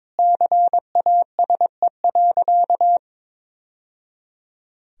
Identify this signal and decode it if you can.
Morse code, heard as a keyed tone, about 22 wpm, 700 Hz. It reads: CASE.
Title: Sentences from Top 1000 Words - Review Part 3 - 22wpm